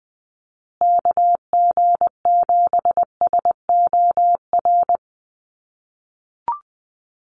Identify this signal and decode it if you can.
Morse code, heard as a keyed tone, about 20 wpm, 700 Hz. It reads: KG7SOR
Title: USA Call Signs Repeated 2x 20wpm